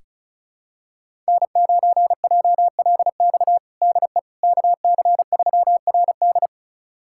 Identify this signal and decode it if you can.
Morse code, heard as a keyed tone, about 36 wpm, 700 Hz. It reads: N9JLX DE KC3RD
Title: USA Callsign DE Callsign 35wpm